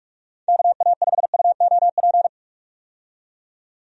Morse code, heard as a keyed tone, about 46 wpm, 700 Hz. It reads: KA5UOP
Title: USA Call Signs - 6 Characters - 45wpm